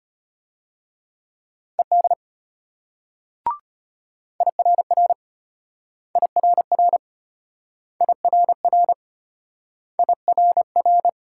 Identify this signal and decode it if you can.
Morse code, heard as a keyed tone, about 34 wpm, 700 Hz. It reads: ED IRR IRR IRR IRR